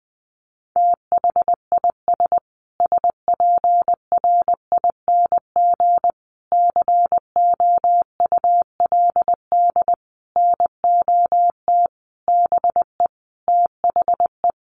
Morse code, heard as a keyed tone, about 20 wpm, 700 Hz. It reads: THIS SPRING COULD NOT BE THE
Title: Sentences from Top 1500 Words 20wpm